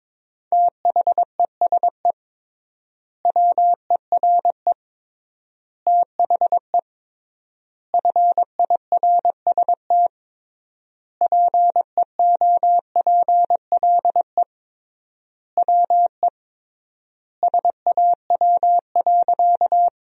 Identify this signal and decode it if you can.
Morse code, heard as a keyed tone, about 21 wpm, 700 Hz. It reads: THESE WERE THE FIRST PEOPLE WE SAW.
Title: 3x Word-Spacing - Sentences from Top 300 Words - 22wpm